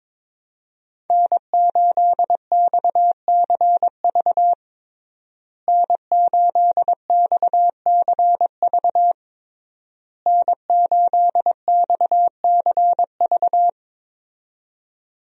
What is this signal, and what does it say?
Morse code, heard as a keyed tone, about 22 wpm, 700 Hz. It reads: N8XCV N8XCV N8XCV
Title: USA Call Signs - 5 Characters - Repeated 3x - 22wpm